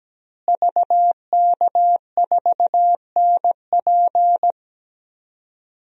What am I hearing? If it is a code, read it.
Morse code, heard as a keyed tone, about 17 wpm, 700 Hz. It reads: VK4NP
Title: DXCC-100 Call Signs - 5 Characters - Repeated 3x - 17wpm